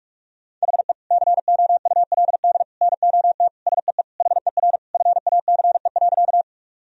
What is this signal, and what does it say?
Morse code, heard as a keyed tone, about 41 wpm, 700 Hz. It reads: HE COULD NOT SEE HER FACE.
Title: Sentences from Top 1000 Words - Review Part 2 - 45wpm